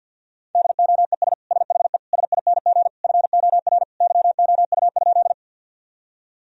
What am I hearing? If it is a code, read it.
Morse code, heard as a keyed tone, about 52 wpm, 700 Hz. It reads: DOES SHE SING FOR YOU?
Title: Sentences from Top 1000 Words - Review Part 2 - 50wpm